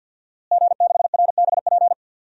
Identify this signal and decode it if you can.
Morse code, heard as a keyed tone, about 51 wpm, 700 Hz. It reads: G6RBP
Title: DXCC-100 Call Signs - 5 Characters - Repeated 2x - 50wpm